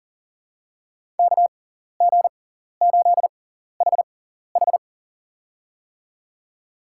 Morse code, heard as a keyed tone, about 39 wpm, spaced as for 14 wpm, 690 Hz. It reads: KG8HH
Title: USA Call Signs - 5 Characters - 6x Character Spacing - 40wpm